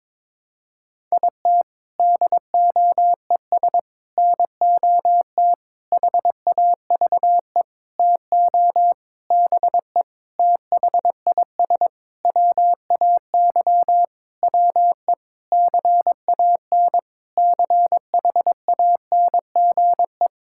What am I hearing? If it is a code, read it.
Morse code, heard as a keyed tone, about 22 wpm, 700 Hz. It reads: IT DOES NOT HAVE TO BE THIS WAY WE CAN CHANGE
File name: Sentences with 7-Word Phrases - 22wpm